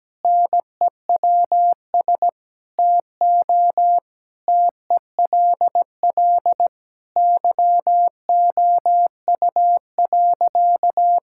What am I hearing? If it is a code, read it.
Morse code, heard as a keyed tone, about 17 wpm, 700 Hz. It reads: NEWS TO TELL YOU.
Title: Sentences from Top 1000 Words - Review Part 5 - 17wpm